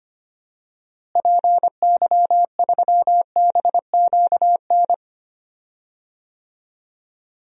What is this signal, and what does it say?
Morse code, heard as a keyed tone, about 25 wpm, 700 Hz. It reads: PY3BQN
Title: DXCC-100 Call Signs - 6 Characters - Repeated 2x - 25wpm